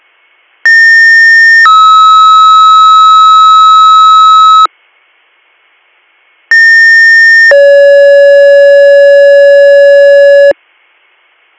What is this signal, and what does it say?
Two-tone page, heard as a one-second tone, then a three-second tone, about 1800 then 1290 Hz, about 1800 then 560 Hz. It is heard from 0.6 to 10.5 s.